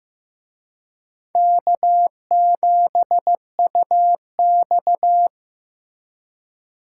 Morse code, heard as a keyed tone, about 18 wpm, 700 Hz. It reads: K7UX
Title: Speed Racing - USA 1x2 Call Signs 15wpm